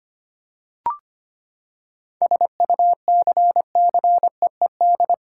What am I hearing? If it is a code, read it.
Morse code, heard as a keyed tone, about 25 wpm, 700 Hz. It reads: SUCCEED
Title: Seven Letter Words - 25wpm